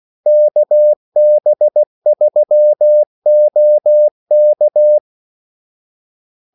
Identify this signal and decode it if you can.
Morse code, heard as a keyed tone, about 16 wpm, 600 Hz. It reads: KB3OK
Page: Kaldesignaler 16 wpm